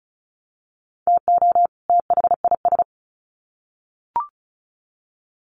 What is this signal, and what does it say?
Morse code, heard as a keyed tone, about 34 wpm, 700 Hz. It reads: TO THIS
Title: Sets of 2 Words 35wpm